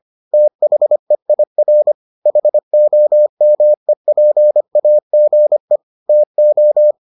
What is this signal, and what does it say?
Morse code, heard as a keyed tone, about 25 wpm, 600 Hz. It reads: THEIR HOMEPAGE TO